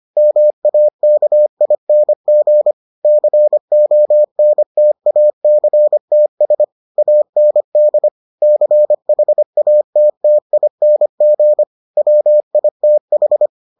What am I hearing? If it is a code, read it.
Morse code, heard as a keyed tone, about 25 wpm, 600 Hz. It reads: MAKING CONTACTS AND CHATTING WITH